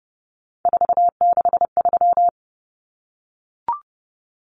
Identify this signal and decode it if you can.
Morse code, heard as a keyed tone, about 30 wpm, 700 Hz. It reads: 463
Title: Common Cut Numbers - 3-Digit 30wpm